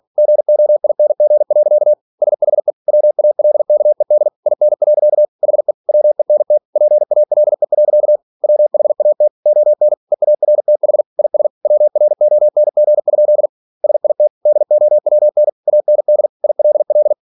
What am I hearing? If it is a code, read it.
Morse code, heard as a keyed tone, about 46 wpm, 600 Hz. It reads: GOING. SHE WALKED IN. HE WENT PALE. WHAT ON EARTH IS WRONG? SIT DOWN AND ILL
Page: Never 47wpm